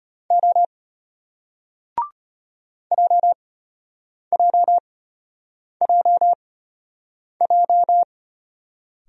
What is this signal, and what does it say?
Morse code, heard as a keyed tone, about 34 wpm, spaced as for 8 wpm, 700 Hz. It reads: O JJJJ